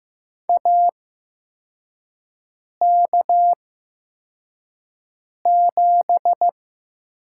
Morse code, heard as a keyed tone, about 15 wpm, spaced as for 4 wpm, 700 Hz. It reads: AK7
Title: USA Call Signs - 4 Characters - 8x Character Spacing - 15wpm